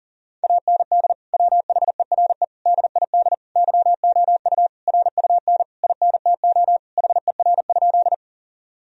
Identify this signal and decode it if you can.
Morse code, heard as a keyed tone, about 40 wpm, 700 Hz. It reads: AND WHERE DID YOU RUN INTO HER?
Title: Sentences from Top 1000 Words - Review Part 1 - 40wpm